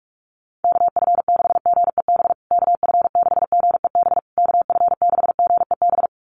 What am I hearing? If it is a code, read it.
Morse code, heard as a keyed tone, about 44 wpm, 700 Hz. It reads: KF6GEB KF6GEB KF6GEB
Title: USA Call Signs Repeated 3x 45wpm